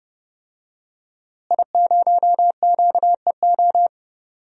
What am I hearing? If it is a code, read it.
Morse code, heard as a keyed tone, about 30 wpm, 700 Hz. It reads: I0QEO
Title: DXCC-100 Call Signs - 5 Characters - Repeated 3x - 30wpm